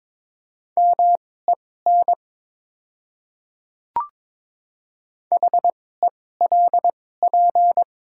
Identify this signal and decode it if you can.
Morse code, heard as a keyed tone, about 22 wpm, spaced as for 16 wpm, 700 Hz. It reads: MEN HELP
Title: 2x Character Spacing - Top 300 Words - 22wpm